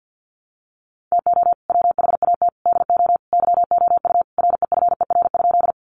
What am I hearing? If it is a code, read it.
Morse code, heard as a keyed tone, about 50 wpm, 700 Hz. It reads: TO WHAT DO YOU REFER?
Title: Sentences from Top 1000 Words 50wpm